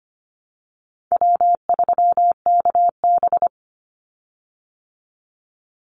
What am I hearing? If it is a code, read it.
Morse code, heard as a keyed tone, about 25 wpm, 700 Hz. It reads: W3KB
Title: Easy Call Signs Repeated 2x - 25wpm